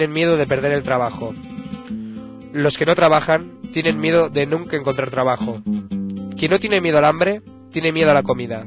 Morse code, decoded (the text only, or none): none